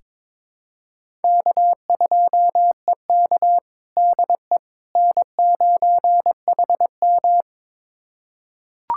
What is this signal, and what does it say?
Morse code, heard as a keyed tone, about 22 wpm, 700 Hz. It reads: K2EK DE N9HM
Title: USA Callsign DE Callsign - 4 Characters - 22wpm